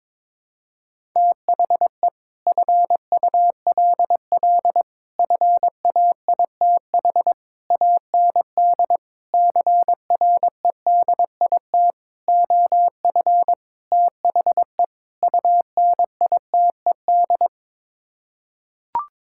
Morse code, heard as a keyed tone, about 21 wpm, 700 Hz. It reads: THE FULL FAITH AND CREDIT OF THE UNITED
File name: practice / 22wpm / Sets of 8 Words - 22wpm